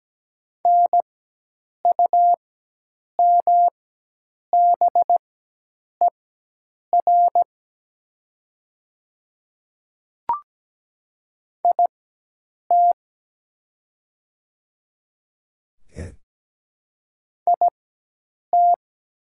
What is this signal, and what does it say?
Morse code, heard as a keyed tone, about 17 wpm, spaced as for 8 wpm, 700 Hz. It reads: NUMBER IT IT
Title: 4x Character Spacing - Top 100 Words - 17wpm